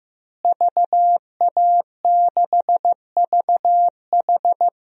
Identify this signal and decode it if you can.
Morse code, heard as a keyed tone, about 16 wpm, 700 Hz. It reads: VA6VH
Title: DXCC-100 Call Signs - 5 Characters - Repeated 3x - 15wpm